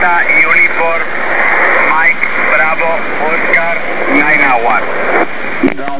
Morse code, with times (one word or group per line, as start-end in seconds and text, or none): none